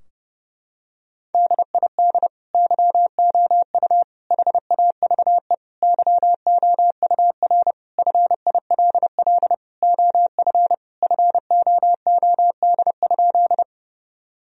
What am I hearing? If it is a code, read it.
Morse code, heard as a keyed tone, about 32 wpm, 700 Hz. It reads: DID YOU HAVE YOUR FILL OF FOOD?
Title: Sentences from Top 1000 Words - Review Part 3 - 30wpm